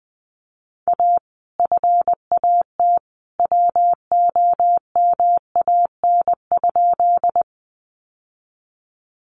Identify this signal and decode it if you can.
Morse code, heard as a keyed tone, about 20 wpm, 700 Hz. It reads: A FAT WOMAN?
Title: Sentences from Top 1100 Words 20wpm